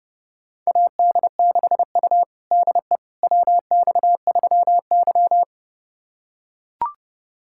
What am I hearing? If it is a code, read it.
Morse code, heard as a keyed tone, about 32 wpm, 700 Hz. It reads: AD6U DE WX3Y
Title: USA Callsign DE Callsign - 4 Characters - 30wpm